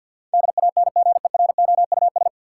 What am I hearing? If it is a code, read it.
Morse code, heard as a keyed tone, about 47 wpm, 700 Hz. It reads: DANGEROUS
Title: Sentences from Top 1200 Words - New Word Drill - 50wpm